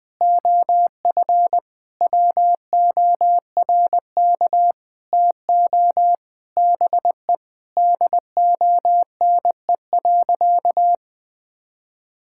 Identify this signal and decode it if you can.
Morse code, heard as a keyed tone, about 19 wpm, 700 Hz. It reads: OF WORK TO BE DONE.
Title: Sentences with 7-Word Phrases - Spoken Beginning - 20wpm